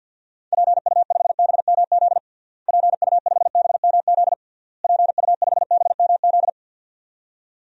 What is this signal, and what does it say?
Morse code, heard as a keyed tone, about 49 wpm, 700 Hz. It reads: PU5BMZ PU5BMZ PU5BMZ
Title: DXCC-100 Call Signs - 6 Characters - Repeated 3x - 50wpm